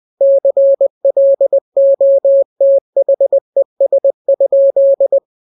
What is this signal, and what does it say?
Morse code, heard as a keyed tone, about 20 wpm, 550 Hz. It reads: CLOTHES?
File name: jokes / joke 19 20 0000